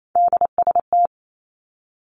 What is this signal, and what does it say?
Morse code, heard as a keyed tone, about 28 wpm, 700 Hz. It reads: DST
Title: Three Challenging Letter-Number Combinations 28wpm